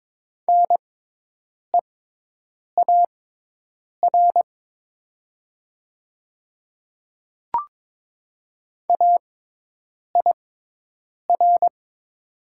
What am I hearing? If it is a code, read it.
Morse code, heard as a keyed tone, about 23 wpm, spaced as for 8 wpm, 700 Hz. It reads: NEAR AIR
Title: 6x Character Spacing - Top 300 Words - 22wpm